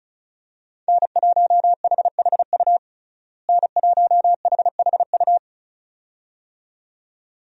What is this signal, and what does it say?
Morse code, heard as a keyed tone, about 34 wpm, 700 Hz. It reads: N1HHU N1HHU
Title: USA Call Signs - 5 Characters - Repeated 2x - 35wpm